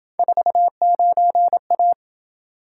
Morse code, heard as a keyed tone, about 27 wpm, 700 Hz. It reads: 49A